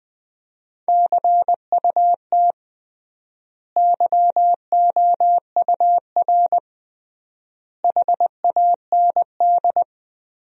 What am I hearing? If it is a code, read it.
Morse code, heard as a keyed tone, about 20 wpm, 700 Hz. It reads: CUT YOUR HAND
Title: 3x Word-Spacing - Sentences from Top 300 Words - 20wpm